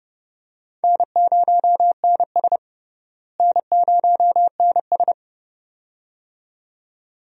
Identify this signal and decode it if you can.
Morse code, heard as a keyed tone, about 30 wpm, 700 Hz. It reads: N0NS N0NS
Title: USA Call Signs - 4 Characters - Repeated 2x - 30wpm